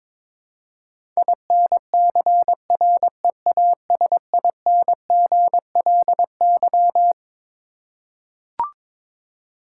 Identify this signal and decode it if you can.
Morse code, heard as a keyed tone, about 22 wpm, 700 Hz. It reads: INCREASINGLY
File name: Twelve Letter Words - 22wpm